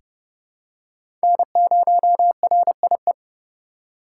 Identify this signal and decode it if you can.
Morse code, heard as a keyed tone, about 31 wpm, 700 Hz. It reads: N0RIE
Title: USA Call Signs - 4x Character Spacing - 30wpm